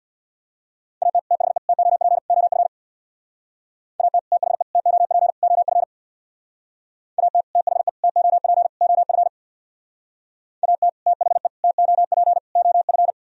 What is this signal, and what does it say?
Morse code, heard as a keyed tone, about 64 wpm, 700 Hz. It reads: AT THE TOP OF AT THE TOP OF AT THE TOP OF AT THE TOP OF